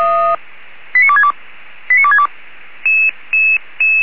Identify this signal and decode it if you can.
Five-tone selective call (ZVEI), heard as quick runs of five tones, tones of about 80 ms. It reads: 89262 89262